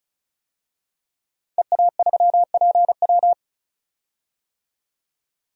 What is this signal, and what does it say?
Morse code, heard as a keyed tone, about 36 wpm, 700 Hz. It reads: EA3PW
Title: DXCC-100 Call Signs - 5 Characters - 35wpm